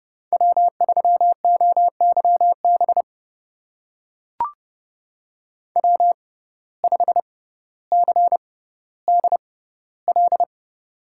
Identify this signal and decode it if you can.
Morse code, heard as a keyed tone, about 30 wpm, spaced as for 15 wpm, 700 Hz. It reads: W3OYB W5CDL
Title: USA Call Signs - 5 Characters - 6x Character Spacing - 30wpm